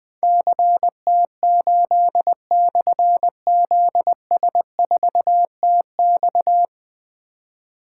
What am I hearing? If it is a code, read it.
Morse code, heard as a keyed tone, about 20 wpm, 700 Hz. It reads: CT8/ZS4TX
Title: Difficult Call Signs 20wpm